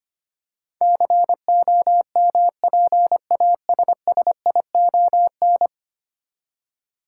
Morse code, heard as a keyed tone, about 25 wpm, 700 Hz. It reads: COMPASSION
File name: 10 Letter Words - First 3 Letters Given - 25wpm